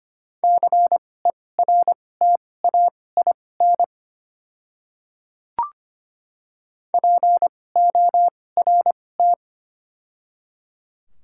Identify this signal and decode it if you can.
Morse code, heard as a keyed tone, about 25 wpm, spaced as for 18 wpm, 700 Hz. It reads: CERTAIN PORT
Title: 2x Character Spacing - Top 500 Words - 25wpm